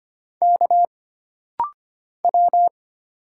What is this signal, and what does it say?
Morse code, heard as a keyed tone, about 25 wpm, 700 Hz. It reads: K W
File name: Single Character (Letter:Number:Punctuation:Prosign) - Rapid-Fire - 25wpm